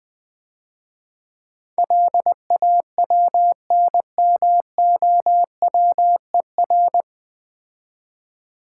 Morse code, heard as a keyed tone, about 20 wpm, 700 Hz. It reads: LAWNMOWER
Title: Easy Words - Speed-Racing with Hints - 20wpm